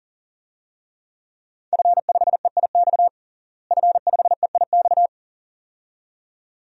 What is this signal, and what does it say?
Morse code, heard as a keyed tone, about 40 wpm, 690 Hz. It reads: F5EIX F5EIX
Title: DXCC-100 Call Signs - 5 Characters - Repeated 2x - 40wpm